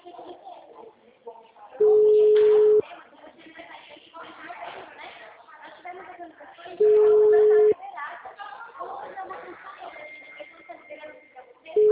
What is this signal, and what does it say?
Ringback tone (Europe, 425 Hz), with speech.